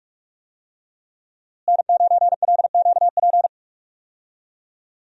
Morse code, heard as a keyed tone, about 46 wpm, 700 Hz. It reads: N9LQP
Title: USA Call Signs - 5 Characters - Repeated 3x - 45wpm